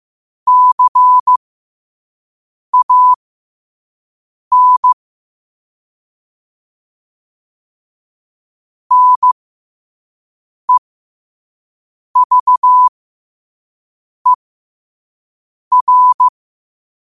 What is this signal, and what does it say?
Morse code, heard as a keyed tone, about 15 wpm, spaced as for 5 wpm, 1000 Hz. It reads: CAN NEVER
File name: qotdfiles / 5wpm